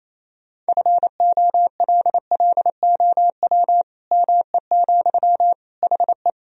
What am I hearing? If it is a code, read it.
Morse code, heard as a keyed tone, about 28 wpm, 700 Hz. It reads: FOLLOW ME, HE
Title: Sentences from Top 1000 Words - Review Part 1 - 28wpm